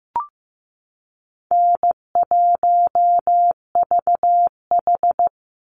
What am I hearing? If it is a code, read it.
Morse code, heard as a keyed tone, about 15 wpm, 700 Hz. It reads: N1VH
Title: Easy Call Signs Repeated 3x - 15wpm